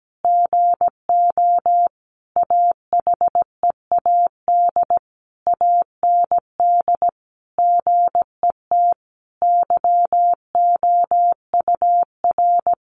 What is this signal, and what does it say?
Morse code, heard as a keyed tone, about 17 wpm, 700 Hz. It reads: GO AHEAD AND GET YOUR
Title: Sentences from Top 1100 Words 17wpm